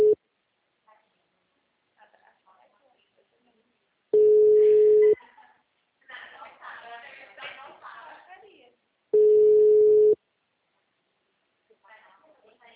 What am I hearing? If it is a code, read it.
Ringback tone (Europe, 425 Hz), with speech.